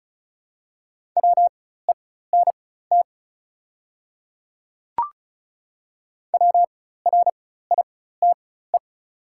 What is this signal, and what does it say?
Morse code, heard as a keyed tone, about 36 wpm, spaced as for 16 wpm, 700 Hz. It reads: WENT WRITE
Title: 4x Character Spacing - Top 300 Words - 35wpm